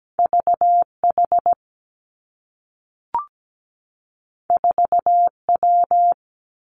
Morse code, heard as a keyed tone, about 17 wpm, 700 Hz. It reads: V H 4 W
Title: Two Challenging Letter-Number Combinations 17wpm